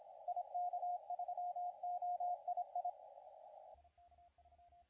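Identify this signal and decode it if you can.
Morse code, heard as a keyed tone, about 26 wpm, 710 Hz, with radio noise. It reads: IK3OII
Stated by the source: CW contest audio recorded at LA4C during the SAC CW contest.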